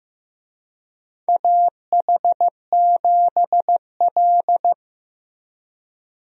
Morse code, heard as a keyed tone, about 15 wpm, 700 Hz. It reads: AH7L
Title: USA Call Signs - 4 Characters - Repeated 3x - 15wpm